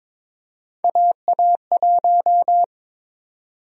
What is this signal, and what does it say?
Morse code, heard as a keyed tone, about 22 wpm, 700 Hz. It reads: AA1
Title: USA Call Sign Prefixes - 22wpm